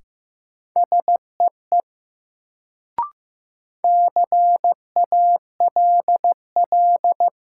Morse code, heard as a keyed tone, about 15 wpm, 700 Hz. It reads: SEE CALL
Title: Sentences from Top 100 Words - New Word Drill - 15wpm